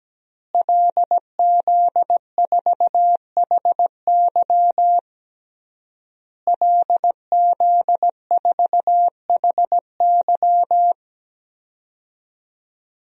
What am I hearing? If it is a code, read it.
Morse code, heard as a keyed tone, about 17 wpm, 700 Hz. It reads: LZ4HY LZ4HY
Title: DXCC-100 Call Signs - 5 Characters - Repeated 3x - 17wpm